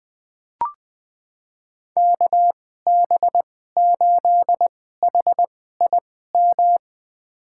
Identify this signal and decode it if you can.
Morse code, heard as a keyed tone, about 20 wpm, spaced as for 14 wpm, 700 Hz. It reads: KB8HIM
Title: USA Call Signs - 6 Characters - 2x Character Spacing - 20wpm